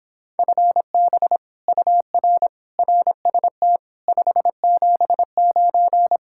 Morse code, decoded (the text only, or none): FB UR RST 579